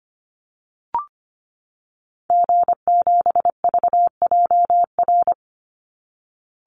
Morse code, heard as a keyed tone, about 25 wpm, 700 Hz. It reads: G7VJR
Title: Call Signs Repeated 3x - 25wpm